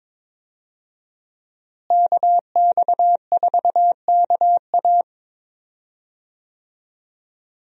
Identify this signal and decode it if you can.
Morse code, heard as a keyed tone, about 22 wpm, 700 Hz. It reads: KX4KA
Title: USA Call Signs - 5 Characters - Repeated 2x - 22wpm